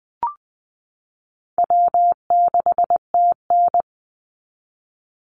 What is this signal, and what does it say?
Morse code, heard as a keyed tone, about 20 wpm, 700 Hz. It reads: W6TN
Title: Easy Call Signs Repeated 3x - 20wpm